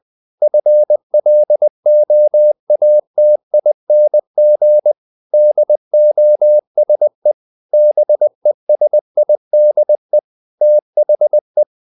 Morse code, heard as a keyed tone, about 20 wpm, 600 Hz. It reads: FLOATING DOSE BESIDE THE
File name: the old man and the sea / the old man and the sea part 1 20wpm